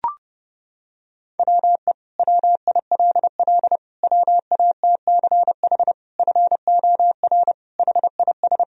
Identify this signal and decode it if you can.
Morse code, heard as a keyed tone, about 30 wpm, 700 Hz. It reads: WE WILL WATCH FOR HIS
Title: Sentences from Top 1000 Words - Review Part 1 - 30wpm